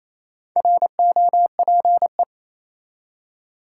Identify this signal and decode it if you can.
Morse code, heard as a keyed tone, about 28 wpm, 700 Hz. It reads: ROPE
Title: Four Letter Words - 28wpm